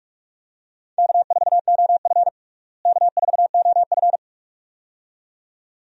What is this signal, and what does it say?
Morse code, heard as a keyed tone, about 46 wpm, 700 Hz. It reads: K4OF K4OF
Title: USA Call Signs - 4 Characters - Repeated 2x - 45wpm